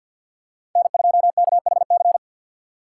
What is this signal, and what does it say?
Morse code, heard as a keyed tone, about 47 wpm, 700 Hz. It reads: N2KHC
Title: USA Call Signs - 5 Characters - 4x Character Spacing - 50wpm